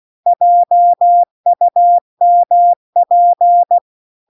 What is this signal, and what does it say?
Morse code, heard as a keyed tone, about 16 wpm, 700 Hz. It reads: JUMP